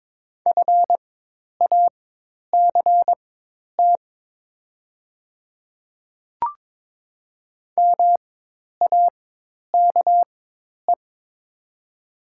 Morse code, heard as a keyed tone, about 22 wpm, spaced as for 10 wpm, 700 Hz. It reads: FACT MAKE
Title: 4x Character Spacing - Top 500 Words - 22wpm